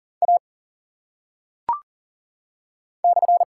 Morse code, heard as a keyed tone, about 42 wpm, 700 Hz. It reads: A /
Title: Top 180 QSO Elements 40wpm